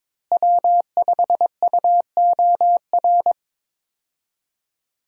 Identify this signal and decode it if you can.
Morse code, heard as a keyed tone, about 21 wpm, 700 Hz. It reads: W5UOR
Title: USA Call Signs - 5 Characters - Repeated 3x - 22wpm